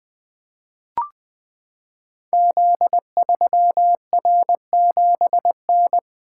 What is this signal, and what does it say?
Morse code, heard as a keyed tone, about 20 wpm, 700 Hz. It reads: Z3R7N
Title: Code Groups - 5 Letter:Numbers - 20wpm